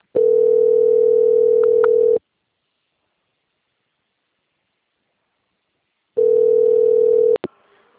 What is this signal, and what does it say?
Ringback tone (North America), 440 + 480 Hz.